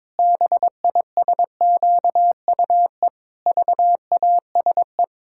Morse code, heard as a keyed tone, about 22 wpm, 700 Hz. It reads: BISQUE VASE
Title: Pangram Challenge v2 - 22wpm